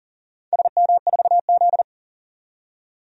Morse code, heard as a keyed tone, about 40 wpm, 700 Hz. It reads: SM4Z
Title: DXCC-100 Call Signs - 4 Characters - 40wpm